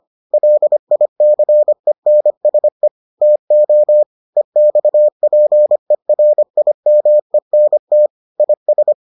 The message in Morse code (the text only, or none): LICENSE TO EXPERIMENT IS